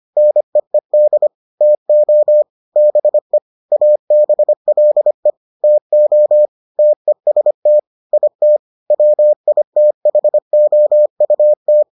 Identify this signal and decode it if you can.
Morse code, heard as a keyed tone, about 25 wpm, 600 Hz. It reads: NEED TO BE ABLE TO TEST IT WITHOUT